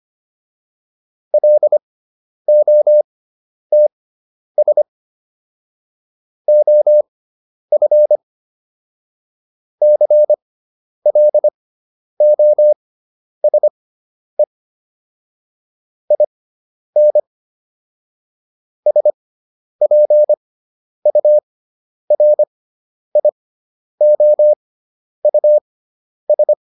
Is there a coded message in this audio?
Yes: Morse code from 1.3 to 26.5 s, heard as a keyed tone, about 25 wpm, spaced as for 10 wpm, 600 Hz. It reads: LOTS OF CLOSE IN SPURIOUS